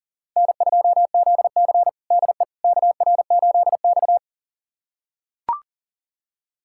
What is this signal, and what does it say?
Morse code, heard as a keyed tone, about 40 wpm, 700 Hz. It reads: N2ZC DE KR8X
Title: USA Callsign DE Callsign - 4 Characters - 40wpm